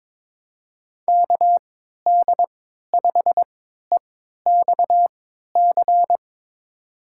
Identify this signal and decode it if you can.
Morse code, heard as a keyed tone, about 22 wpm, spaced as for 13 wpm, 700 Hz. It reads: KD5EXC
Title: USA Call Signs - 3x Character Spacing - 22wpm